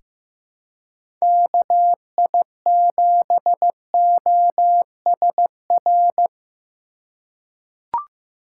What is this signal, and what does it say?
Morse code, heard as a keyed tone, about 15 wpm, 700 Hz. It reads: KI7OSR
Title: USA Call Signs - 6 Characters - 3x Character Spacing - 15wpm